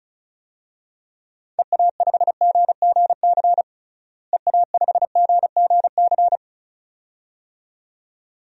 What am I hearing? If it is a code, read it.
Morse code, heard as a keyed tone, about 36 wpm, 700 Hz. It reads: EA5GGC EA5GGC
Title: DXCC-100 Call Signs - 6 Characters - Repeated 2x - 35wpm